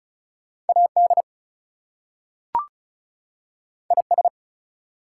Morse code, heard as a keyed tone, about 36 wpm, 700 Hz. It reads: AD IS
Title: Two Letter Words - 35wpm